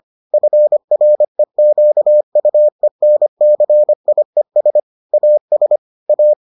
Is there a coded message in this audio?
Yes: Morse code from 0.3 to 6.3 s, heard as a keyed tone, about 25 wpm, 600 Hz. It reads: FREQUENCIES AS A